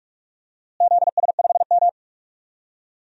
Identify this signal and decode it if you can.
Morse code, heard as a keyed tone, about 42 wpm, 700 Hz. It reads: ZS5M
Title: DXCC-100 Call Signs - 4 Characters - Repeated 2x - 45wpm